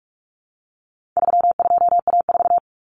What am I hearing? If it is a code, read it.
Morse code, heard as a keyed tone, about 44 wpm, 700 Hz. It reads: 32A4
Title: Common Cut Numbers - 4-Digit 45wpm